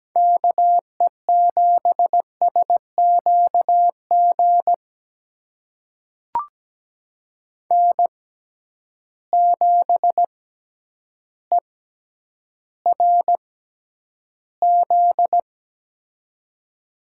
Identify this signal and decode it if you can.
Morse code, heard as a keyed tone, about 17 wpm, spaced as for 9 wpm, 700 Hz. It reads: KE7SQG N7ERZ
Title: USA Call Signs - 6x Character Spacing - 17wpm